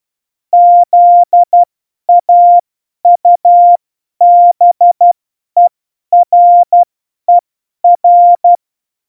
Morse code, heard as a keyed tone, about 12 wpm, spaced as for 10 wpm, 700 Hz. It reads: ZAUBERER
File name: Morsekurs / Witze / Witz 03 April 2018, 50 F70